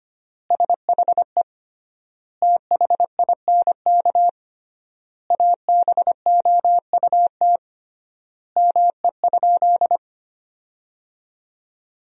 Morse code, heard as a keyed tone, about 25 wpm, 700 Hz. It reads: SHE THINK ABOUT ME?
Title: 3x Word-Spacing - Sentences from Top 200 Words - 25wpm